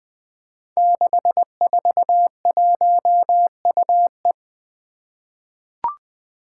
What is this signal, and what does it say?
Morse code, heard as a keyed tone, about 20 wpm, 700 Hz. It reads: 641UE
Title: Five Challenging Letter-Number Combinations 20wpm